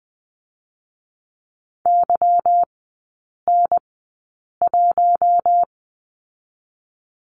Morse code, heard as a keyed tone, about 20 wpm, 700 Hz. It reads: Y N 1
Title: 172 Lesson 28e - Number 9 - Three Characters 20wpm